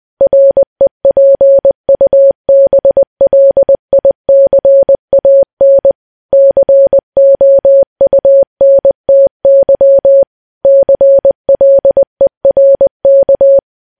Morse code, heard as a keyed tone, about 20 wpm, 550 Hz. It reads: REPUBLICAN COUNTY CLERK
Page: news-headlines-in-morse-code-at-20-wpm